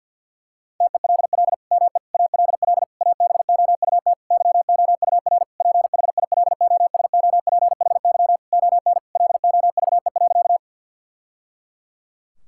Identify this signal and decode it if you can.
Morse code, heard as a keyed tone, about 52 wpm, 700 Hz. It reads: TELL ME ALL ABOUT YOUR PHILOSOPHY ON LOVE.
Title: Sentences from Top 1000 Words - Review Part 6 - 50wpm